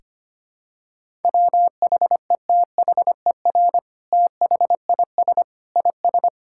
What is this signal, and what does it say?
Morse code, heard as a keyed tone, about 25 wpm, 700 Hz. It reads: WHETHER THIS IS
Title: Sentences from Top 1000 Words - Review Part 5 - 25wpm